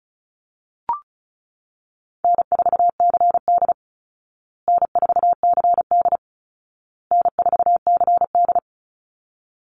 Morse code, heard as a keyed tone, about 34 wpm, 700 Hz. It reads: N4CD N4CD N4CD
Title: Call Signs Repeated 2x - 35wpm